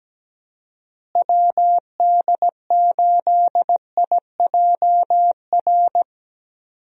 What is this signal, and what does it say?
Morse code, heard as a keyed tone, about 17 wpm, 700 Hz. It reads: WD8IJR
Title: USA Call Signs - 6 Characters - Repeated 3x - 17wpm